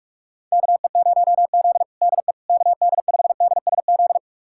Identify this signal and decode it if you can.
Morse code, heard as a keyed tone, about 46 wpm, 700 Hz. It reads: KE0Z DE KD5DSZ
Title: USA Callsign DE Callsign 45wpm